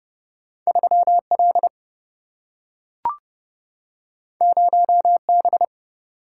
Morse code, heard as a keyed tone, about 30 wpm, 700 Hz. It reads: 3L 0B
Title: Single-Letter Single-Number 30wpm